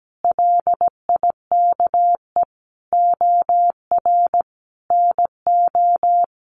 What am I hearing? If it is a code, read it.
Morse code, heard as a keyed tone, about 17 wpm, 700 Hz. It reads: LIKE OR NO
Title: Sentences from Top 1000 Words 17wpm